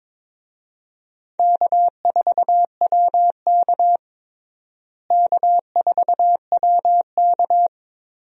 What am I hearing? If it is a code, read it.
Morse code, heard as a keyed tone, about 22 wpm, 700 Hz. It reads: K4WK K4WK
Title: USA Call Signs - 4 Characters - Repeated 2x - 22wpm